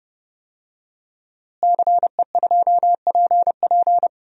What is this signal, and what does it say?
Morse code, heard as a keyed tone, about 30 wpm, 700 Hz. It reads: CE2PP
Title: DXCC-100 Call Signs - 5 Characters - Repeated 3x - 30wpm